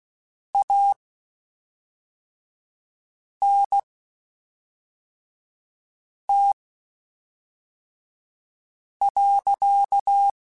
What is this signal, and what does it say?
Morse code, heard as a keyed tone, about 16 wpm, spaced as for 3 wpm, 800 Hz. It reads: ANT.
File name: (spaced) / QSOs Spaced-mp3s / WD0FBT 1 @5 sp